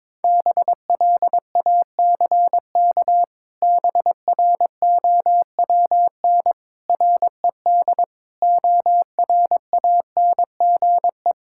Morse code, heard as a keyed tone, about 22 wpm, 700 Hz. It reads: BLACK BROWN RED ORANGE
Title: Pangram Challenge v2 - 22wpm